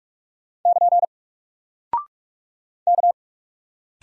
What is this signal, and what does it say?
Morse code, heard as a keyed tone, about 46 wpm, 700 Hz. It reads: <KN> K
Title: Single Character (Letter:Number:Punctuation:Prosign) 45wpm